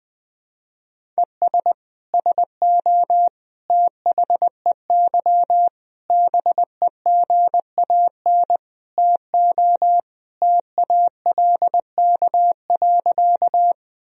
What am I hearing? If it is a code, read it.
Morse code, heard as a keyed tone, about 20 wpm, 700 Hz. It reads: ES SO THEY BEGAN TO TALK.
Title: Sentences from Top 400 Words and Abbreviations 20wpm